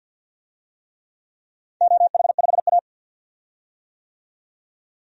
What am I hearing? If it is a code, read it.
Morse code, heard as a keyed tone, about 52 wpm, 700 Hz. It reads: OH5A
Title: DXCC-100 Call Signs - 4 Characters - 50wpm